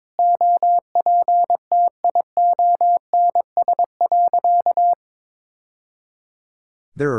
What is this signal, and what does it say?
Morse code, heard as a keyed tone, about 22 wpm, 700 Hz. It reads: OPTIONS.
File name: practice / 22wpm / Sentences with 3-Word Phrases - 22wpm